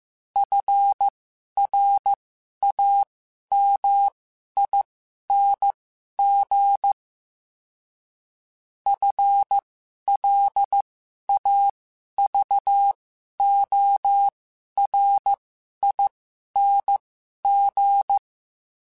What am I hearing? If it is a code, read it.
Morse code, heard as a keyed tone, about 15 wpm, spaced as for 11 wpm, 800 Hz. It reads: FRAMING FLAVORING
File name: Special INGs-mp3s / Special INGs-mp3s / Sngl Ing @10